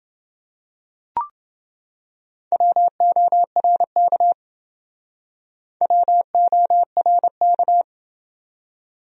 Morse code, heard as a keyed tone, about 28 wpm, 700 Hz. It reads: WORK WORK